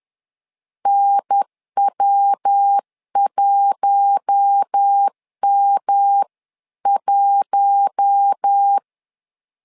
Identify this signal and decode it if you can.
Morse code, heard as a keyed tone, about 10 wpm, 800 Hz. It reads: NW1M1
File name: LEZIONI / 4 Q1 10wpm 2448